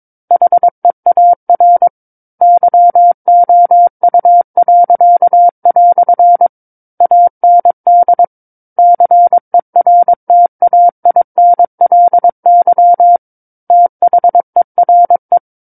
Morse code, heard as a keyed tone, about 22 wpm, 700 Hz. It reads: HEAR YOU." AND CERTAINLY THERE
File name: 06-Chapter-06 22 WPM